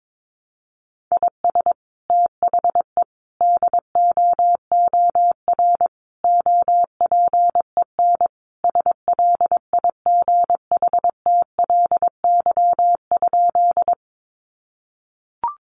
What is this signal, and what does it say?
Morse code, heard as a keyed tone, about 22 wpm, 700 Hz. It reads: IS THE DOOR OPEN SLIGHTLY?
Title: Sentences from Top 1300 Words 22wpm